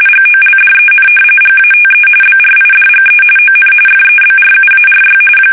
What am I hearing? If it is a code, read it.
RTTY, about 45.45 baud. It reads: GOLDEN GATE BRIDGE ON FEBRUARY
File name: fsk850 / battleshipnj